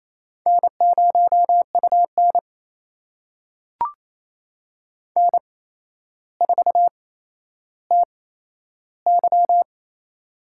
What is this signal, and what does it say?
Morse code, heard as a keyed tone, about 28 wpm, spaced as for 12 wpm, 700 Hz. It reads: N0UN N4TY
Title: USA Call Signs - 4 Characters - 8x Character Spacing - 28wpm